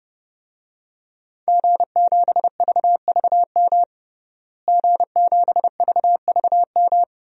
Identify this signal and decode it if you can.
Morse code, heard as a keyed tone, about 30 wpm, 700 Hz. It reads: G7VVM G7VVM
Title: DXCC-100 Call Signs - 5 Characters - Repeated 3x - 30wpm